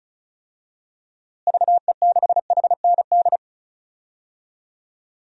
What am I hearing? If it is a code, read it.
Morse code, heard as a keyed tone, about 36 wpm, 700 Hz. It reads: VE6HND
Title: DXCC-100 Call Signs - 6 Characters - Repeated 2x - 35wpm